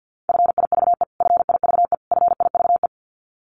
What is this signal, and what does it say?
Morse code, heard as a keyed tone, about 50 wpm, 700 Hz. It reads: FIVE FIVE FIVE
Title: Top 500 Words Repeated 50wpm